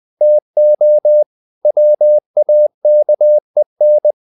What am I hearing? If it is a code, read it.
Morse code, heard as a keyed tone, about 20 wpm, 600 Hz. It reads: TO WAKEN
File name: the old man and the sea / the old man and the sea part 1 20wpm